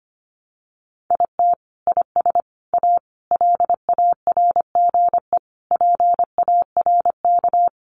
Morse code, heard as a keyed tone, about 25 wpm, 700 Hz. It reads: IT IS A LARGE PARK
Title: Sentences from Top 1300 Words 25wpm